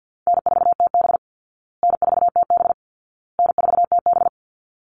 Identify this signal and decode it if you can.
Morse code, heard as a keyed tone, about 50 wpm, 700 Hz. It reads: N4TB N4TB N4TB
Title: Call Signs Repeated 2x - 50wpm